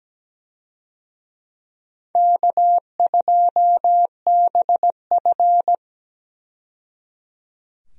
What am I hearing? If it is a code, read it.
Morse code, heard as a keyed tone, about 17 wpm, 700 Hz. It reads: K2BF
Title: USA Call Signs - 4 Characters - 17wpm